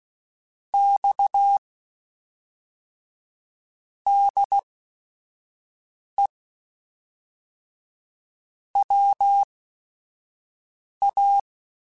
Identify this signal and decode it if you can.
Morse code, heard as a keyed tone, about 16 wpm, spaced as for 5 wpm, 800 Hz. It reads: X DE WA
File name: QSOs / QSOs-mp3s / WA7QOX 1@5